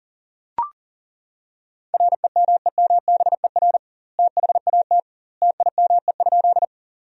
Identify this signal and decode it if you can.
Morse code, heard as a keyed tone, about 37 wpm, 700 Hz. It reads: REMEMBER THAT TIME?
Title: 2x Word-Spacing - Sentences from Top 400 Words - 40wpm